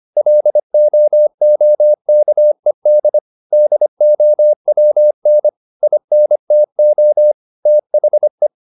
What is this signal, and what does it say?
Morse code, heard as a keyed tone, about 25 wpm, 600 Hz. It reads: LOOKED DOWN INTO THE
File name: the old man and the sea part 1 25wpm